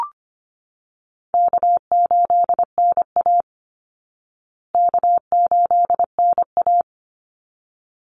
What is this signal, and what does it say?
Morse code, heard as a keyed tone, about 25 wpm, 700 Hz. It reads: K8NA K8NA
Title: Easy Call Signs Repeated 2x - 25wpm